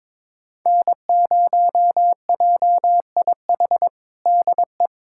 Morse code, heard as a keyed tone, about 22 wpm, 700 Hz. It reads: N0JIH DE
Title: USA Callsign DE Callsign - 5 Characters - 22wpm